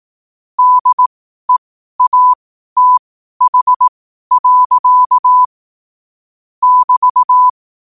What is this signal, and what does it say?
Morse code, heard as a keyed tone, about 18 wpm, spaced as for 12 wpm, 1000 Hz. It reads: DEATH. <BT>
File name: qotdfiles / 12wpm